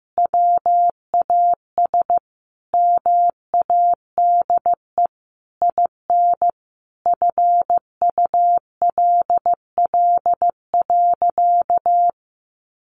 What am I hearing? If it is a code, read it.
Morse code, heard as a keyed tone, about 15 wpm, 700 Hz. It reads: WAS MADE IN FULL.
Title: Sentences from Top 1200 Words 15wpm